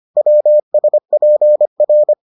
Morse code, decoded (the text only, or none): WSPR